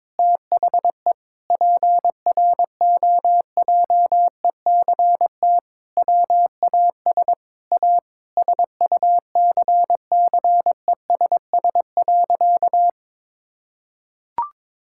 Morse code, heard as a keyed tone, about 22 wpm, 700 Hz. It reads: THE PROJECT WAS A SUCCESS.
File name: practice / 22wpm / Sentences with 8-Word Phrases - 22wpm